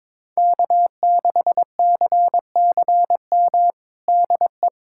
Morse code, heard as a keyed tone, about 22 wpm, 700 Hz. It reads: K6CCM DE
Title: USA Callsign DE Callsign - 5 Characters - 22wpm